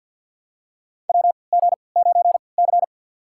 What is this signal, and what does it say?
Morse code, heard as a keyed tone, about 51 wpm, spaced as for 28 wpm, 700 Hz. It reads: WG9C
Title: USA Call Signs - 4 Characters - 3x Character Spacing - 50wpm